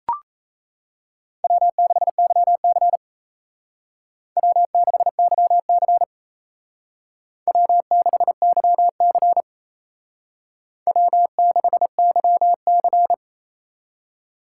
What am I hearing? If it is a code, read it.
Morse code, heard as a keyed tone, about 35 wpm, 700 Hz. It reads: W6YC W6YC W6YC W6YC